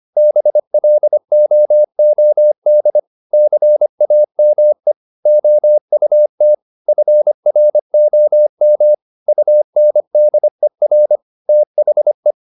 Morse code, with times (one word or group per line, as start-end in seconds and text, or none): 0.2-3.0 BLOOD
3.3-4.9 CAME
5.3-6.5 OUT
6.9-8.9 FROM
9.3-11.2 UNDER
11.5-12.3 THE